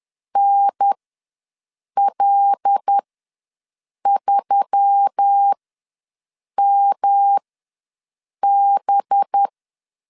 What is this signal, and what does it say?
Morse code, heard as a keyed tone, about 10 wpm, spaced as for 6 wpm, 800 Hz. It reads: NL3MB
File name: LEZIONI / 5 LBVC / 5 LB ws 10wpm 3616